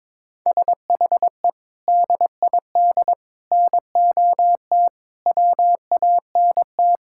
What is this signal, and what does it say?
Morse code, heard as a keyed tone, about 22 wpm, 700 Hz. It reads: SHE DID NOT WANT
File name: practice / 22wpm / Sentences with 4-Word Phrases - 22wpm